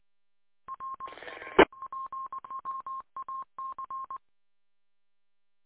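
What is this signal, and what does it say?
Morse code, heard as a keyed tone, about 20 wpm, 1070 Hz. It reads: R EPWAC
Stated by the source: Morse heard on WA8MAC 147.200 repeater audio archive.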